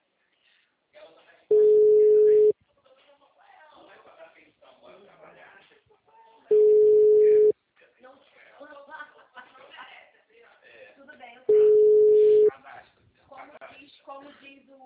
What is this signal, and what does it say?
Ringback tone (Europe, 425 Hz), with speech.